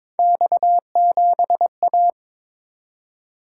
Morse code, heard as a keyed tone, about 23 wpm, 700 Hz. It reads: X7A
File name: practice / 22wpm / Three Letter-Number Combinations - 22wpm